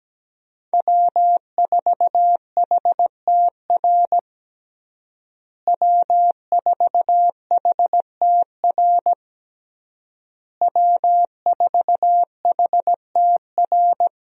Morse code, heard as a keyed tone, about 17 wpm, 700 Hz. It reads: W4HTR W4HTR W4HTR
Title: USA Call Signs - 5 Characters - Repeated 3x - 17wpm